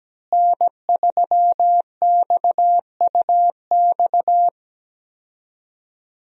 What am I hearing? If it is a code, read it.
Morse code, heard as a keyed tone, about 17 wpm, 700 Hz. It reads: N3XUX
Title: USA Call Signs - 5 Characters - Repeated 2x - 17wpm